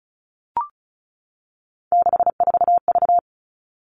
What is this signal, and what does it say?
Morse code, heard as a keyed tone, about 34 wpm, 700 Hz. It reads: 64V
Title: All Cut Numbers - 3-Digit 35wpm